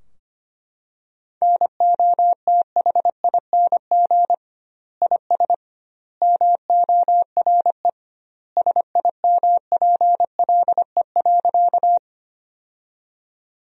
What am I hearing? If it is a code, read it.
Morse code, heard as a keyed tone, about 25 wpm, 700 Hz. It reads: NOTHING IS MORE SIMPLE.
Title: 2x Word-Spacing - Sentences from Top 500 Words - 25wpm